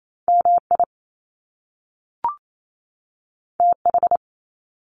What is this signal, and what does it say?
Morse code, heard as a keyed tone, about 28 wpm, 700 Hz. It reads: MI TH
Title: Top Two Letter Combinations in Words 28wpm